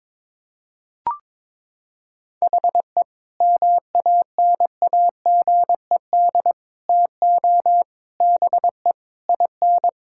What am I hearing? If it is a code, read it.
Morse code, heard as a keyed tone, about 22 wpm, 700 Hz. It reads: HE MANAGED TO BE IN
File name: Sentences with 9-Word Phrases - 22wpm